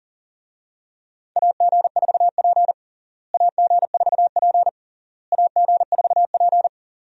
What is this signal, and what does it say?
Morse code, heard as a keyed tone, about 40 wpm, 700 Hz. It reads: AG4P AG4P AG4P
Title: USA Call Signs - 4 Characters - Repeated 3x - 40wpm